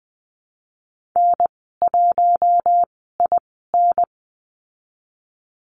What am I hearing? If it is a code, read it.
Morse code, heard as a keyed tone, about 20 wpm, spaced as for 15 wpm, 700 Hz. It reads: N1IN
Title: 069 Lesson 12g - Letter d - Call Signs 20wpm